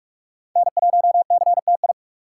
Morse code, heard as a keyed tone, about 40 wpm, 700 Hz. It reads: N1CTI